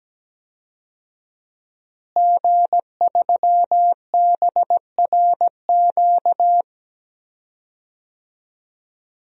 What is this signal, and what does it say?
Morse code, heard as a keyed tone, about 17 wpm, 700 Hz. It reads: G3BRQ
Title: DXCC-100 Call Signs - 5 Characters - Repeated 2x - 17wpm